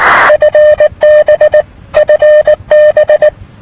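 Morse code, heard as a keyed tone, about 19 wpm, 600 Hz. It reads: FB FB
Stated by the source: Japanese station heard in Rome via multi-hop sporadic-E.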